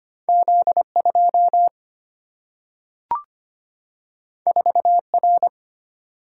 Morse code, heard as a keyed tone, about 25 wpm, 700 Hz. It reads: Z2 4R
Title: Single-Letter Single-Number 25wpm